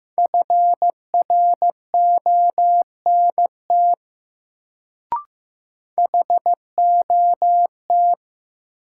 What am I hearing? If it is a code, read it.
Morse code, heard as a keyed tone, about 15 wpm, 700 Hz. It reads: FRONT HOT
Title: Sentences from Top 500 Words - New Word Drill - 15wpm